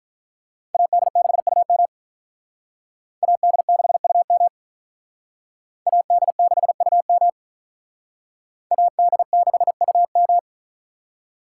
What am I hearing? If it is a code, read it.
Morse code, heard as a keyed tone, about 44 wpm, 700 Hz. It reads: AD6UM AD6UM AD6UM AD6UM